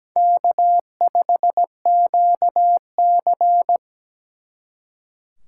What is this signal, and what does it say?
Morse code, heard as a keyed tone, about 20 wpm, 700 Hz. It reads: K5QC
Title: Speed Racing - USA Call Signs 17wpm